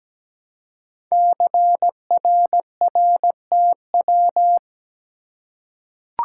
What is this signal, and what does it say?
Morse code, heard as a keyed tone, about 17 wpm, 700 Hz. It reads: CRRTW
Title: Code Groups - 5 Letters - 17wpm